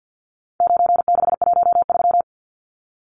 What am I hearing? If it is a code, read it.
Morse code, heard as a keyed tone, about 50 wpm, 700 Hz. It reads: 9613
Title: Numbers - 4-Digit 50wpm